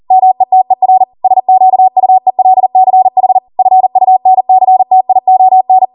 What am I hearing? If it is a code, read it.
Morse code, heard as a keyed tone, about 40 wpm, 750 Hz. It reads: METER SQUELCH FUNCTION
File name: Morse / 40WPM